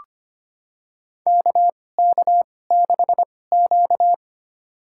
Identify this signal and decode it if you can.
Morse code, heard as a keyed tone, about 25 wpm, spaced as for 18 wpm, 700 Hz. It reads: KK6Q
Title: USA Call Signs - 2x Character Spacing - 25wpm